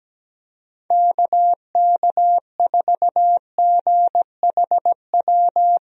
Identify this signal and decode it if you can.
Morse code, heard as a keyed tone, about 17 wpm, 700 Hz. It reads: KK4GHW
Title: USA Call Signs - 6 Characters - 17wpm